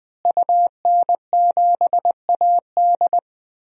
Morse code, heard as a keyed tone, about 20 wpm, 700 Hz. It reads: UN7AD
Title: DXCC-100 Call Signs - 5 Characters - Repeated 3x - 20wpm